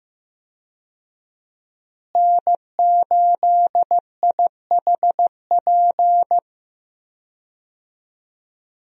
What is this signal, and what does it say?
Morse code, heard as a keyed tone, about 15 wpm, 700 Hz. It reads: N8IHP
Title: USA Call Signs - 5 Characters - Repeated 3x - 15wpm